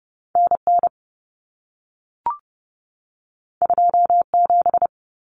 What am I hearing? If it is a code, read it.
Morse code, heard as a keyed tone, about 30 wpm, 700 Hz. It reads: NN 27
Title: Common Cut Numbers - 2-Digit 30wpm